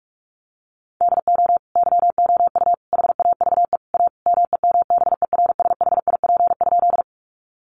Morse code, heard as a keyed tone, about 44 wpm, 700 Hz. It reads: DO YOU HAVE A MEMBERSHIP?
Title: Sentences from Top 1400 Words 45wpm